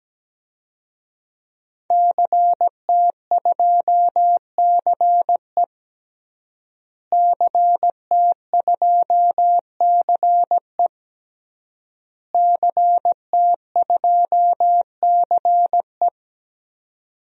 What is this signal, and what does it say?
Morse code, heard as a keyed tone, about 17 wpm, 700 Hz. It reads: CT2CE CT2CE CT2CE
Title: DXCC-100 Call Signs - 5 Characters - Repeated 3x - 17wpm